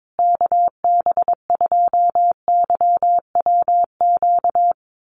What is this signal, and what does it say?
Morse code, heard as a keyed tone, about 22 wpm, 700 Hz. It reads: KB2YWQ
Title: USA Call Signs Repeated 2x 22wpm